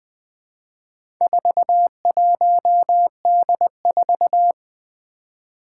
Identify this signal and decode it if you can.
Morse code, heard as a keyed tone, about 20 wpm, 700 Hz. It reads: 41D4
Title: Four Challenging Letter-Number Combinations 20wpm